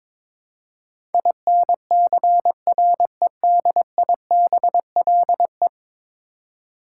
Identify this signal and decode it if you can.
Morse code, heard as a keyed tone, about 22 wpm, 700 Hz. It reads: INCREDIBLE
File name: practice / 22wpm / 10 Letter Words - First 3 Letters Given - 22wpm